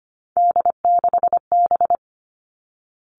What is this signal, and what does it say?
Morse code, heard as a keyed tone, about 25 wpm, 700 Hz. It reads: D6B
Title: Three Challenging Letter-Number Combinations 25wpm